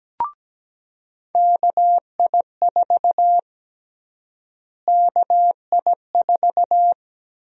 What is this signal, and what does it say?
Morse code, heard as a keyed tone, about 17 wpm, 700 Hz. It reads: KI4 KI4
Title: USA Call Sign Prefixes - Repeated 2x - 17wpm